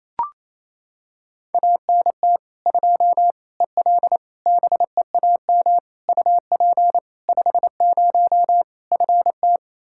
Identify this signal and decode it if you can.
Morse code, heard as a keyed tone, about 28 wpm, 700 Hz. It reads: ANT 2 EL BEAM UP 50 FT
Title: Practice QSO - Exchange 2 - Info - 28wpm